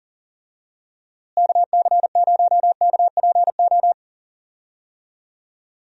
Morse code, heard as a keyed tone, about 40 wpm, 700 Hz. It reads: KC0KPO
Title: USA Call Signs - 6 Characters - Repeated 3x - 40wpm